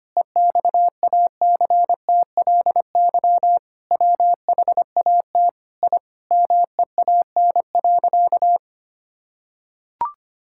Morse code, heard as a keyed tone, about 25 wpm, 700 Hz. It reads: EXACTLY WHAT I MEAN.
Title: Sentences from Top 1000 Words - Review Part 6 - 25wpm